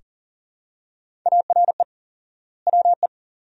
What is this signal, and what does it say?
Morse code, heard as a keyed tone, about 37 wpm, 700 Hz. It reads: ARE WE
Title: 4x Word-Spacing - Sentences from Top 300 Words - 40wpm